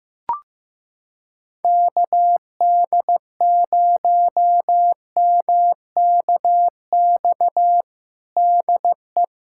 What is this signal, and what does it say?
Morse code, heard as a keyed tone, about 15 wpm, 700 Hz. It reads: KD0MKX DE
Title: USA Callsign DE Callsign - 6 Characters - 15wpm